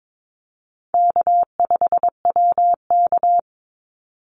Morse code, heard as a keyed tone, about 22 wpm, 700 Hz. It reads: K5WK
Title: Easy Call Signs Repeated 3x - 22wpm